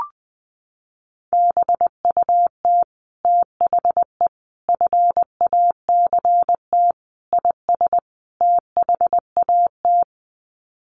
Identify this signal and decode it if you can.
Morse code, heard as a keyed tone, about 20 wpm, 700 Hz. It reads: BUT THE FACT IS THAT
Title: Sets of 5 Words Encore2 20wpm